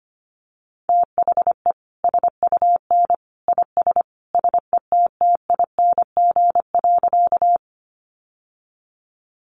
Morse code, heard as a keyed tone, about 25 wpm, 700 Hz. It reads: THE SUN IS SETTING.
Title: Sentences from Top 1300 Words 25wpm